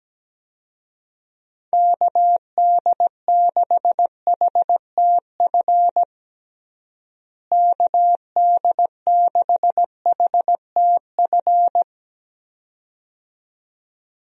Morse code, heard as a keyed tone, about 17 wpm, 700 Hz. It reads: KD6HTF KD6HTF
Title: USA Call Signs - 6 Characters - Repeated 2x - 17wpm